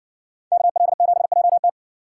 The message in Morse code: KL7JT

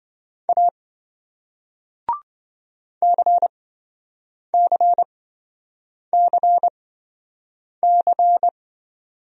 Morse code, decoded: A CCCC

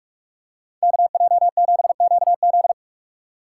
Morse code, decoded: KJ7QZ

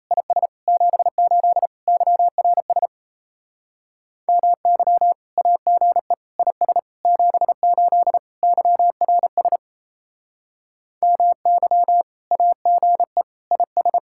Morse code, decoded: IS 78 YRS MY AGE IS 78 YRS MY AGE IS

